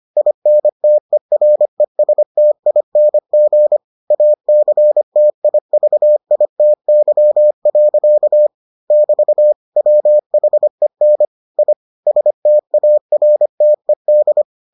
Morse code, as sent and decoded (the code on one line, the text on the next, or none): .. -. - . .-. ... - .. -. --. / .- -.-. - .. ...- .. - -.-- .-.-.- / -...- / .-- .... -. / .. / ... - .- .-. - . -..
INTERESTING ACTIVITY. <BT> WHEN I STARTED